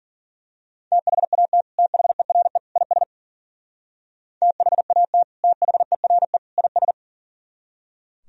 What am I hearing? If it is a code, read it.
Morse code, heard as a keyed tone, about 40 wpm, 700 Hz. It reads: THAT THERE IS THAT THERE IS
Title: Speed-Racing - Sets of 3 Words 40wpm